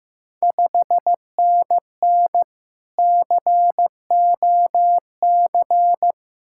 Morse code, heard as a keyed tone, about 15 wpm, 700 Hz. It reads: 5NN COC
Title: Ontario QSO Party Exchange 15wpm